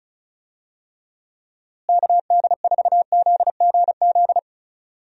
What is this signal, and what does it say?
Morse code, heard as a keyed tone, about 34 wpm, 700 Hz. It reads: KD4ZGZ
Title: USA Call Signs - 6 Characters - Repeated 3x - 35wpm